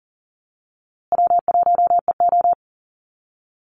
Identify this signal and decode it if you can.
Morse code, heard as a keyed tone, about 40 wpm, 700 Hz. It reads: W1EO
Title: USA Call Signs Repeated 3x 40wpm